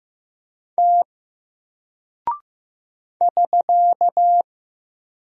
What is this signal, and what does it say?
Morse code, heard as a keyed tone, about 15 wpm, 700 Hz. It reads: T <SK>
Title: Single Character (Letter:Number:Punctuation:Prosign) 15wpm